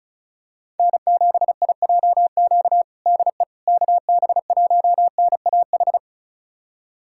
Morse code, heard as a keyed tone, about 34 wpm, 700 Hz. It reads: N7IJQ DE KB1NAH
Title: USA Callsign DE Callsign 35wpm